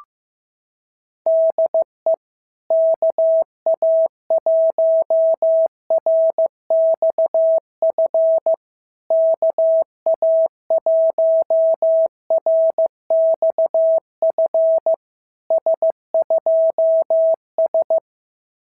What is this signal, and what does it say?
Morse code, heard as a keyed tone, about 14 wpm, 650 Hz. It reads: DE KA1RXF KA1RXF S2S
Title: SOTA QSO Practice 15wpm